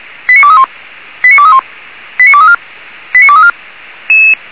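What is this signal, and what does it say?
Five-tone selective call (ZVEI), heard as quick runs of five tones, tones of about 70 ms. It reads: 89231 89231 89234 89234
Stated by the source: ZVEI five-tone selective call 89240 received by SDR.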